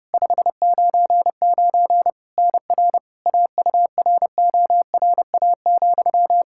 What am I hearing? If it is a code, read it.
Morse code, heard as a keyed tone, about 30 wpm, 700 Hz. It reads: 599 NR AURORA,
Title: K1USN SST QSO Practice 30wpm